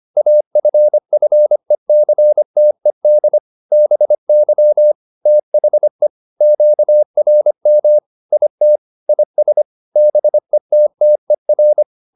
Morse code, heard as a keyed tone, about 25 wpm, 600 Hz. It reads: AFFECTED BY THE QRM IT IS BETTER